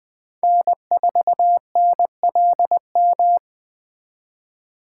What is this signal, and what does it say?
Morse code, heard as a keyed tone, about 20 wpm, 700 Hz. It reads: N4NLM
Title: Speed Racing - USA 1x3 Call Signs 17wpm